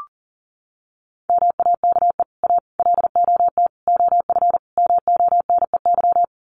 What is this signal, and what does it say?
Morse code, heard as a keyed tone, about 40 wpm, 700 Hz. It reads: MAKE A LOT OF MONEY
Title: Sets of 5 Words Encore2 40wpm